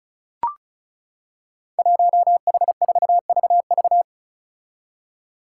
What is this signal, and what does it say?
Morse code, heard as a keyed tone, about 34 wpm, 700 Hz. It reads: 1H4VV
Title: Five Challenging Letter-Number Combinations 35wpm